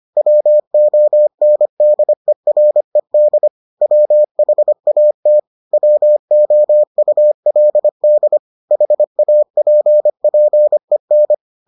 Morse code, heard as a keyed tone, about 25 wpm, 600 Hz. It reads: WONDERED WHAT WOULD HAPPEN